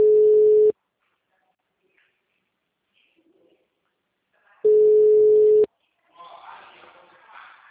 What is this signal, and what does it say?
Ringback tone (Europe, 425 Hz), with speech.